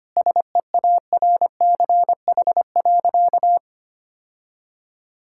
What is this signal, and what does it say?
Morse code, heard as a keyed tone, about 25 wpm, 700 Hz. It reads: SEARCH.
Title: Sentences from Top 1000 Words - Review Part 5 - 25wpm